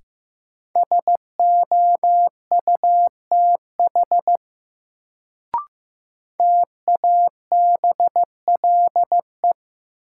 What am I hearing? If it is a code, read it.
Morse code, heard as a keyed tone, about 15 wpm, 700 Hz. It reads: SOUTH TABLE
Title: Sentences from Top 400 Words - New Word Drill - 15wpm